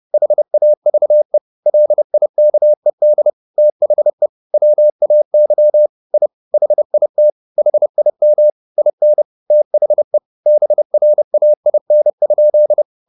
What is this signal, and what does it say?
Morse code, heard as a keyed tone, about 30 wpm, 600 Hz. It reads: HAVE LIKED THE WAY I HIT HIM IN THE BRAIN?